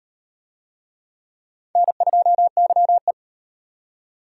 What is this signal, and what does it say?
Morse code, heard as a keyed tone, about 34 wpm, 700 Hz. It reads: N2YE